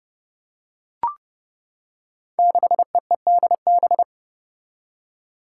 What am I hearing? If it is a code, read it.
Morse code, heard as a keyed tone, about 30 wpm, 700 Hz. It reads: 6EEDB
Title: Five Challenging Letter-Number Combinations 30wpm